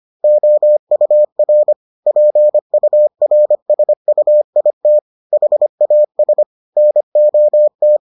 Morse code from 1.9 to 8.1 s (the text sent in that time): PURSUIT HAS NOT